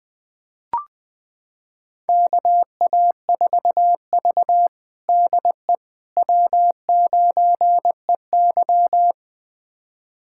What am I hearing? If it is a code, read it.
Morse code, heard as a keyed tone, about 21 wpm, 700 Hz. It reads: KA4V DE W9EY
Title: USA Callsign DE Callsign - 4 Characters - 20wpm